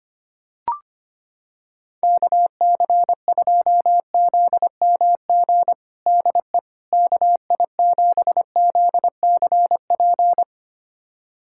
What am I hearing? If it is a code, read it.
Morse code, heard as a keyed tone, about 25 wpm, 700 Hz. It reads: KC2ZMG DE KI7ZCP
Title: USA Callsign DE Callsign - 6 Characters - 25wpm